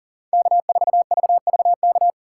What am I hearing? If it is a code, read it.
Morse code, heard as a keyed tone, about 40 wpm, 700 Hz. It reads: K4VVK
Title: USA Call Signs - 5 Characters - 6x Character Spacing - 40wpm